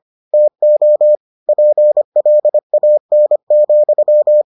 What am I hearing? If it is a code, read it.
Morse code, heard as a keyed tone, about 25 wpm, 600 Hz. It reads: TO PLAN,